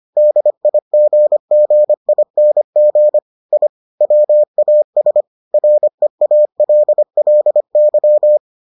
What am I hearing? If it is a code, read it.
Morse code, heard as a keyed tone, about 25 wpm, 600 Hz. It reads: DIGGING I WAS REALLY